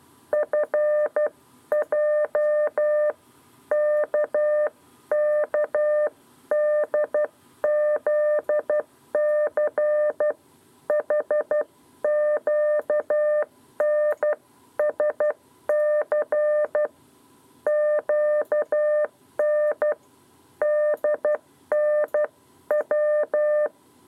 Morse code, heard as a keyed tone, about 12 wpm, spaced as for 10 wpm, 600 Hz. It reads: FJ KKDZC HQNSC QNDNW
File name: 7b lezione cw 2